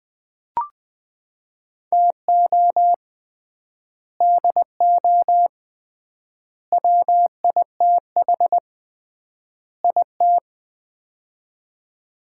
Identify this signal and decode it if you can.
Morse code, heard as a keyed tone, about 20 wpm, 700 Hz. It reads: TO DO WITH IT
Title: 3x Word-Spacing - Sets of 4 Words - 20wpm